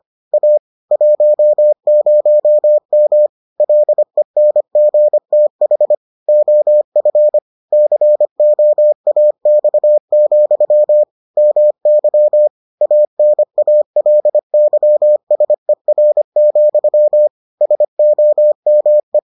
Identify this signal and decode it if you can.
Morse code, heard as a keyed tone, about 25 wpm, 600 Hz. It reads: A 10M LENGTH OF COAX, MY ANALYSER, SOME